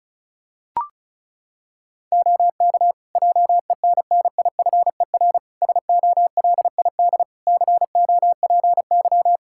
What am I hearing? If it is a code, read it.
Morse code, heard as a keyed tone, about 36 wpm, 700 Hz. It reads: OK JENNIFER SOLID COPY
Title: Practice QSO - Exchange 3 - More Info - 35wpm